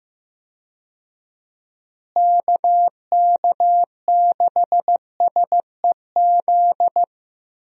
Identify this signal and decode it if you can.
Morse code, heard as a keyed tone, about 15 wpm, 700 Hz. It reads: KK6SEZ
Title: USA Call Signs - 6 Characters - Repeated 2x - 15wpm